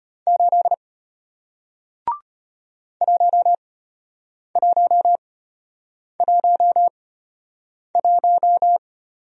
Morse code, heard as a keyed tone, about 33 wpm, spaced as for 8 wpm, 700 Hz. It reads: 8 1111